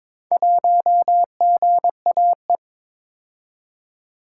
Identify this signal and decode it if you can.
Morse code, heard as a keyed tone, about 22 wpm, 700 Hz. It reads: 1GAE
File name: practice / 22wpm / Four Letter-Number Combinations - 22wpm